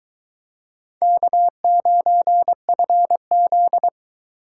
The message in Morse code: K9FZ

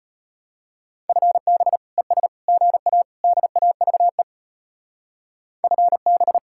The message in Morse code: FB ES GA DAVE FB

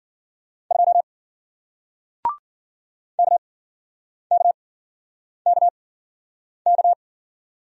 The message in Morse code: 2 KKKK

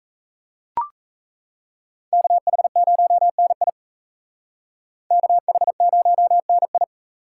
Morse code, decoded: KH0NI KH0NI

